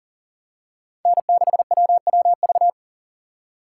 Morse code, heard as a keyed tone, about 40 wpm, 700 Hz. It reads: N6WWV
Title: USA Call Signs - 5 Characters - 6x Character Spacing - 40wpm